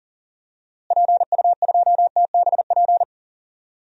Morse code, heard as a keyed tone, about 40 wpm, 700 Hz. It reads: PU2TBP
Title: DXCC-100 Call Signs - 6 Characters - Repeated 2x - 40wpm